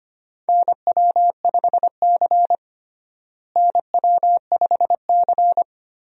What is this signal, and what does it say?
Morse code, heard as a keyed tone, about 25 wpm, 700 Hz. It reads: NW5C NW5C
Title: USA Call Signs - 4 Characters - Repeated 3x - 25wpm